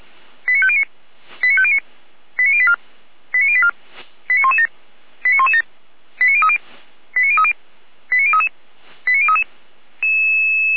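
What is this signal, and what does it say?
Five-tone selective call (ZVEI), heard as quick runs of five tones, tones of about 70 ms. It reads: 89509 89509 89084 89084 89117 89117 89030 89030 89033 89033
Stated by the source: ZVEI five-tone selective call 89057, received by SDR.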